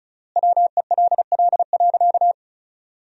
Morse code, heard as a keyed tone, about 36 wpm, 700 Hz. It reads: WELL.
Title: 3x Word-Spacing - Sentences from Top 300 Words - 35wpm